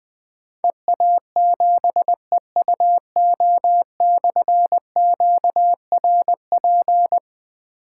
Morse code, heard as a keyed tone, about 20 wpm, 700 Hz. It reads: EA7EUO/QRP
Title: Difficult Call Signs 20wpm